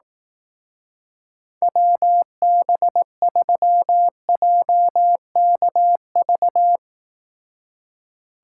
Morse code, heard as a keyed tone, about 18 wpm, 700 Hz. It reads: WB3JKV
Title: Speed Racing - USA 2x3 Call Signs 15wpm